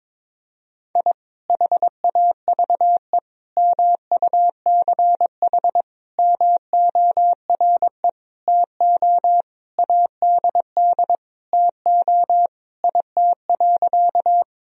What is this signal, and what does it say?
Morse code, heard as a keyed tone, about 22 wpm, 700 Hz. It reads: I HAVE MUCH MORE TO ADD TO IT.
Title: Sentences from Top 1000 Words - Review Part 1 - 22wpm